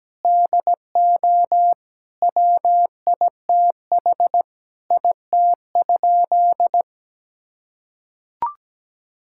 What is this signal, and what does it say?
Morse code, heard as a keyed tone, about 17 wpm, 700 Hz. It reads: DO WITH IT?
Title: Sentences from Top 1600 Words 17wpm